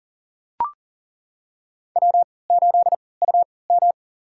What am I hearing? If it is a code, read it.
Morse code, heard as a keyed tone, about 40 wpm, spaced as for 23 wpm, 700 Hz. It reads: W8UM
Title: USA Call Signs - 3x Character Spacing - 40wpm